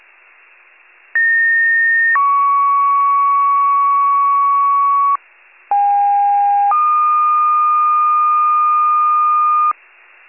Two-tone page, heard as a one-second tone, then a three-second tone, about 1850 then 1120 Hz, about 810 then 1220 Hz. It is heard from 1.1 to 9.7 s.
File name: wcalert